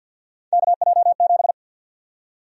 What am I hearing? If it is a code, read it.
Morse code, heard as a keyed tone, about 53 wpm, 700 Hz. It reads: KJ7
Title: USA Call Sign Prefixes - 50wpm